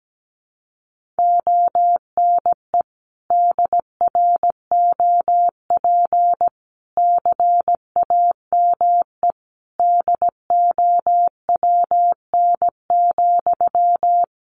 Morse code, heard as a keyed tone, about 17 wpm, 700 Hz. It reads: ONE DROP CAME DOWN,
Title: Sentences from Top 1400 Words 17wpm